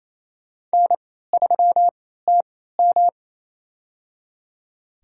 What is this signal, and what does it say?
Morse code, heard as a keyed tone, about 28 wpm, spaced as for 16 wpm, 700 Hz. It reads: N3TM
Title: USA Call Signs - 4 Characters - 3x Character Spacing - 28wpm